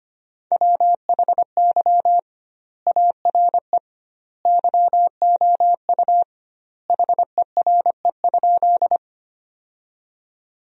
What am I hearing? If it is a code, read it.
Morse code, heard as a keyed tone, about 25 wpm, 700 Hz. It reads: WHY ARE YOU HERE?
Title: 2x Word-Spacing - Sentences from Top 200 Words - 25wpm